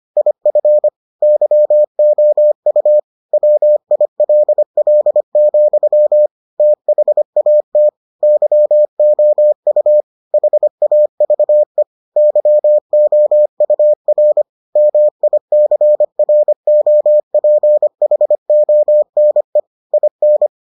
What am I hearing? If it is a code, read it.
Morse code, heard as a keyed tone, about 25 wpm, 600 Hz. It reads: IF YOU WILL, THAT YOU HAVE YOUR MICROPHONE IN